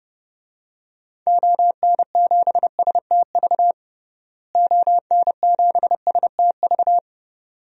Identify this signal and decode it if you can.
Morse code, heard as a keyed tone, about 30 wpm, 700 Hz. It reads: ON7STV ON7STV
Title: DXCC-100 Call Signs - 6 Characters - Repeated 2x - 30wpm